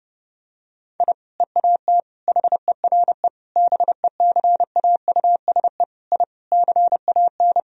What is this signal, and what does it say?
Morse code, heard as a keyed tone, about 30 wpm, 700 Hz. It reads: I EAT HERE BECAUSE I CAN
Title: Sentences from Top 1000 Words - Review Part 3 - 30wpm